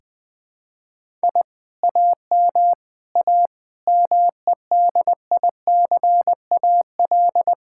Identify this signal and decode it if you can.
Morse code, heard as a keyed tone, about 19 wpm, 700 Hz. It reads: I AM A MEDICAL
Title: Sentences from Top 1000 Words - Review Part 5 - 20wpm